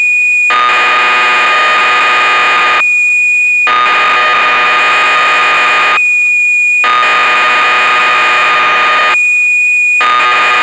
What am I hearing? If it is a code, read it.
Cassette data (Kansas City type).
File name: magazines / Acorn User / 1985-06 BBC Programs